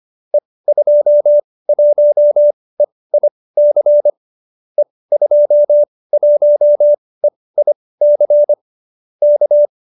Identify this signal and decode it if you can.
Morse code, heard as a keyed tone, about 25 wpm, spaced as for 18 wpm, 570 Hz. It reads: E21EIC E21EIC K
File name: qso208b 18